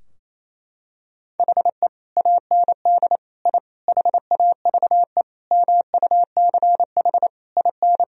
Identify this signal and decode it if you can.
Morse code, heard as a keyed tone, about 27 wpm, 700 Hz. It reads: HE AND I HAVE MUCH IN
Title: Sentences from Top 1000 Words - Review Part 3 - 28wpm